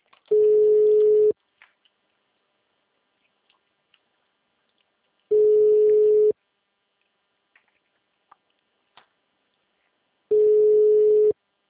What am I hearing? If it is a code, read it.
Ringback tone (Europe, 425 Hz), with speech.